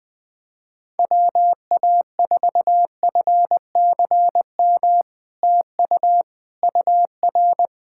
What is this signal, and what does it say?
Morse code, heard as a keyed tone, about 20 wpm, 700 Hz. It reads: WA4FCM TU UR
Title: SOTA QSO Practice 20wpm